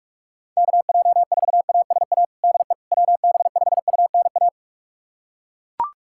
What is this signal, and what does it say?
Morse code, heard as a keyed tone, about 41 wpm, 700 Hz. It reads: KJ4ASA DE WB5UNA
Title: USA Callsign DE Callsign - 6 Characters - 45wpm